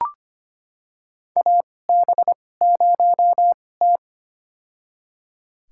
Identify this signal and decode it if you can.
Morse code, heard as a keyed tone, about 25 wpm, spaced as for 18 wpm, 700 Hz. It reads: AB0T
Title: USA Call Signs - 4 Characters - 2x Character Spacing - 25wpm